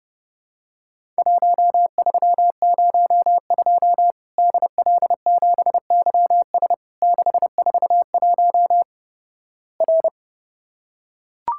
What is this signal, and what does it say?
Morse code, heard as a keyed tone, about 31 wpm, 700 Hz. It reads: 1302 DL7YS 641 R
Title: Worked All Europe DX Contest - QTC Exchange - 30wpm